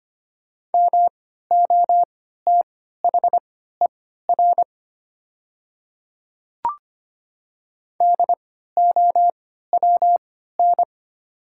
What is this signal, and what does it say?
Morse code, heard as a keyed tone, about 25 wpm, spaced as for 14 wpm, 700 Hz. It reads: MOTHER DOWN
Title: 3x Character Spacing - Top 200 Words - 25wpm